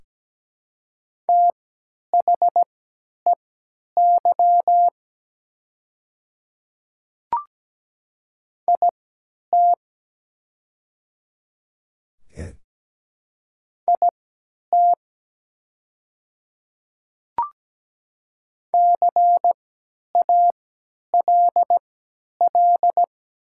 Morse code, heard as a keyed tone, about 17 wpm, spaced as for 10 wpm, 700 Hz. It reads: THEY IT IT CALL
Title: 3x Character Spacing - Top 100 Words - 17wpm